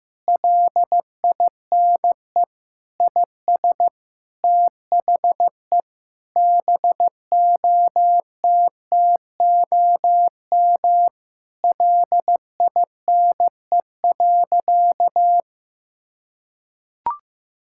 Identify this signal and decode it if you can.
Morse code, heard as a keyed tone, about 16 wpm, 700 Hz. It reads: LINE IS THE BOTTOM LINE.
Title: Sentences from Top 1000 Words - Review Part 6 - 15wpm